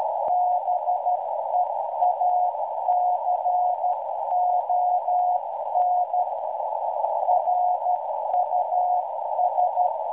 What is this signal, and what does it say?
Morse code, heard as a keyed tone, about 13 wpm, 720 Hz, with noise. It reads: BEACON RUS